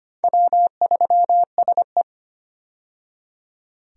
Morse code, heard as a keyed tone, about 25 wpm, 700 Hz. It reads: W3SE
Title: USA Call Signs - 4 Characters - 25wpm